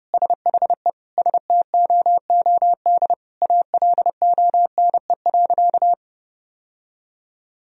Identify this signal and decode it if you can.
Morse code, heard as a keyed tone, about 32 wpm, 700 Hz. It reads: SHE STOOD ALONE.
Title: Sentences from Top 1000 Words - Review Part 2 - 30wpm